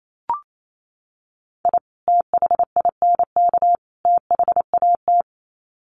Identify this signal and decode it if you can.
Morse code, heard as a keyed tone, about 28 wpm, 700 Hz. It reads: I THINK THAT
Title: Sets of 3 Words 28wpm